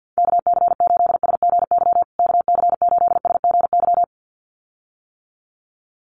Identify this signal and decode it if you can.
Morse code, heard as a keyed tone, about 50 wpm, 700 Hz. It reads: KC8SGY KC8SGY
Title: USA Call Signs Repeated 2x 50wpm